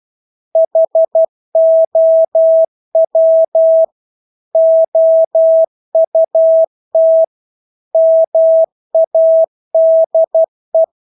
Morse code, heard as a keyed tone, about 12 wpm, 650 Hz. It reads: HOW OUT MADE
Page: Engelske ord 12 wpm